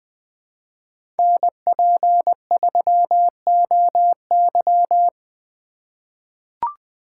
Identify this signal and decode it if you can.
Morse code, heard as a keyed tone, about 20 wpm, 700 Hz. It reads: NP3OY
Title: USA Call Signs - 2x Character Spacing - 20wpm